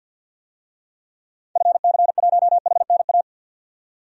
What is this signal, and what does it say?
Morse code, heard as a keyed tone, about 52 wpm, 700 Hz. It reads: FC1HNA
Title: DXCC-100 Call Signs - 6 Characters - 50wpm